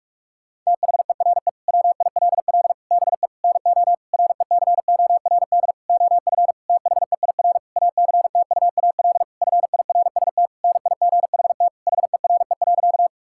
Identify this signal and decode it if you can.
Morse code, heard as a keyed tone, about 46 wpm, 700 Hz. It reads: THERE WILL BE NO RECORD OF THEIR ACTUAL FIRST NIGHT HERE.
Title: Sentences from Top 1000 Words - Review Part 5 - 45wpm